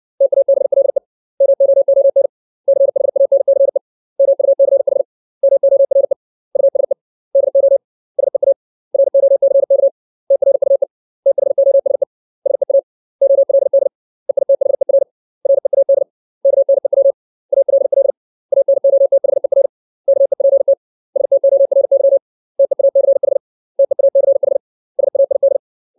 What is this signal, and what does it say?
Morse code, heard as a keyed tone, about 60 wpm, 550 Hz. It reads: TABLE MOON CHANCE MUCH MORE USE DO SEA ROCK TRUE THOSE SEA OLD EITHER READ KNEW ALL ANOTHER KEPT STORY TEACH TEACH SAID
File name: 632 words / 632 Words 60WPM